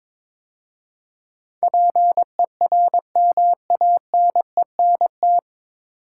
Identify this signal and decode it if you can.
Morse code, heard as a keyed tone, about 22 wpm, 700 Hz. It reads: PERMANENT
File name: 9 Letter Words - First 3 Letters Given - 22wpm